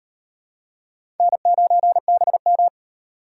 Morse code, heard as a keyed tone, about 36 wpm, 700 Hz. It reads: N9BM